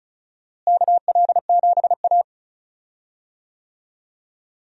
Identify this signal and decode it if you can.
Morse code, heard as a keyed tone, about 36 wpm, 700 Hz. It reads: KL7A
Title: DXCC-100 Call Signs - 4 Characters - Repeated 3x - 35wpm